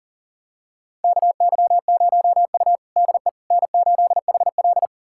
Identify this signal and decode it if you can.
Morse code, heard as a keyed tone, about 40 wpm, 700 Hz. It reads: KY0U DE N8HL
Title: USA Callsign DE Callsign - 4 Characters - 40wpm